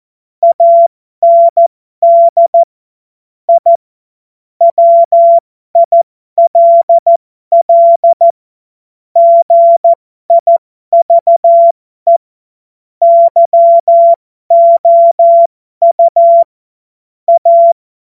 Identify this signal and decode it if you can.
Morse code, heard as a keyed tone, about 14 wpm, spaced as for 12 wpm, 680 Hz. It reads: AND I WILL GIVE YOU A
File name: Morsekurs / Witze